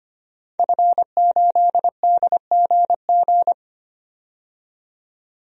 Morse code, heard as a keyed tone, about 25 wpm, 700 Hz. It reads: F8DGG
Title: DXCC-100 Call Signs - 5 Characters - 25wpm